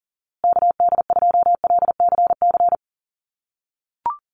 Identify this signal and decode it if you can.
Morse code, heard as a keyed tone, about 40 wpm, 700 Hz. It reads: KD2LCC
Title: USA Call Signs Repeated 2x 40wpm